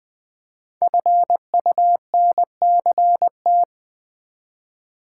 Morse code, heard as a keyed tone, about 20 wpm, 700 Hz. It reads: FUNCT
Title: Build A Word - 5 Letters At A Time - 10-Letter Words - 20wpm